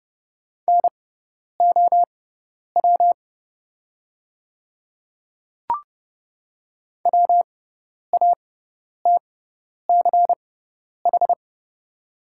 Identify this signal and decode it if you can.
Morse code, heard as a keyed tone, about 30 wpm, spaced as for 10 wpm, 690 Hz. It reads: NOW WATCH
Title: 6x Character Spacing - Top 500 Words - 30wpm